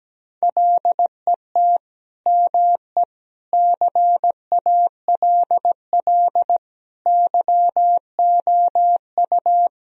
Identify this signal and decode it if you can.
Morse code, heard as a keyed tone, about 17 wpm, 700 Hz. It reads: LET ME CALL YOU
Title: Sentences from Top 1000 Words - Review Part 2 - 17wpm